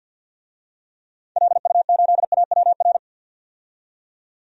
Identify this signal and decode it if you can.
Morse code, heard as a keyed tone, about 47 wpm, 700 Hz. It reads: LU8AWR
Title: DXCC-100 Call Signs - 6 Characters - 50wpm